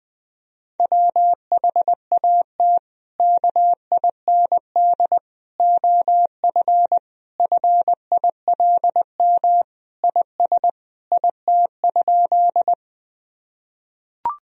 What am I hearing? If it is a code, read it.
Morse code, heard as a keyed tone, about 20 wpm, 700 Hz. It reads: WHAT KIND OF FILM IS IT?
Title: Sentences from Top 1000 Words - Review Part 3 - 20wpm